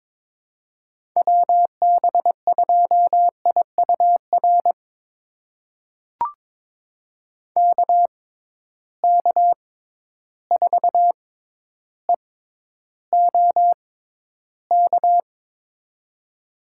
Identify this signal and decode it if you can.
Morse code, heard as a keyed tone, about 22 wpm, spaced as for 11 wpm, 700 Hz. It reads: WB2IUR KK4EOK
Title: USA Call Signs - 6 Characters - 6x Character Spacing - 22wpm